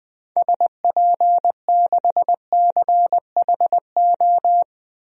Morse code, heard as a keyed tone, about 20 wpm, 700 Hz. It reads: SP6CHO
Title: DXCC-100 Call Signs - 6 Characters - Repeated 3x - 20wpm